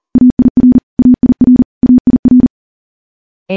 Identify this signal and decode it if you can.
Morse code, heard as a keyed tone, about 43 wpm, 250 Hz. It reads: AIR AIR AIR
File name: morse / morse-training-40WPM